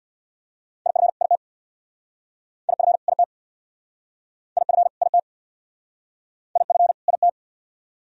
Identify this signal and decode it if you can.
Morse code, heard as a keyed tone, about 63 wpm, 700 Hz. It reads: IF IT IF IT IF IT IF IT